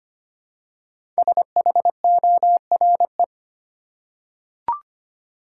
Morse code, heard as a keyed tone, about 25 wpm, 700 Hz. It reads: SHORE
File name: 5 Letter Words - First 3 Letters Given - 25wpm